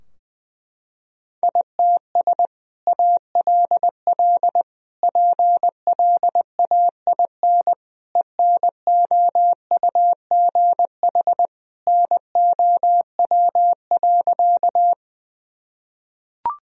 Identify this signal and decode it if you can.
Morse code, heard as a keyed tone, about 20 wpm, 700 Hz. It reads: ITS ALL PLAIN ENOUGH NOW.
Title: Sentences from Top 1000 Words - Review Part 2 - 20wpm